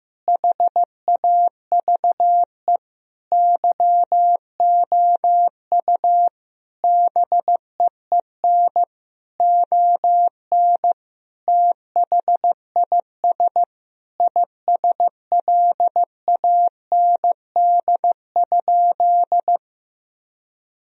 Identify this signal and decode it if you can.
Morse code, heard as a keyed tone, about 15 wpm, 700 Hz. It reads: HAVE YOU BEEN ON THIS ISLAND?
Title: Sentences from Top 1000 Words - Review Part 3 - 15wpm